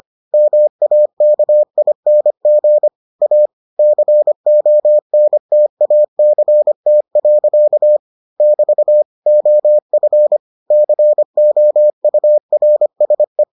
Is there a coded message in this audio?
Yes: Morse code from 0.3 to 13.4 s, heard as a keyed tone, about 25 wpm, 600 Hz. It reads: MAKING A CONTACT. <BT> OF COURSE